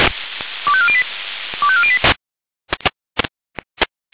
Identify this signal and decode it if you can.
Five-tone selective call (ZVEI), heard as quick runs of five tones, tones of about 70 ms. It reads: 25408 25408